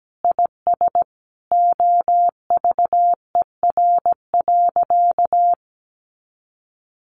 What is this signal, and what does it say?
Morse code, heard as a keyed tone, about 17 wpm, 700 Hz. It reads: IS OVER.
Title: Sentences from Top 1500 Words 17wpm